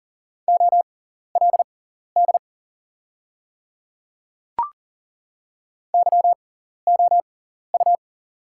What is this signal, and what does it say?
Morse code, heard as a keyed tone, about 40 wpm, spaced as for 14 wpm, 700 Hz. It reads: OLD YOU
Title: 6x Character Spacing - Top 200 Words - 40wpm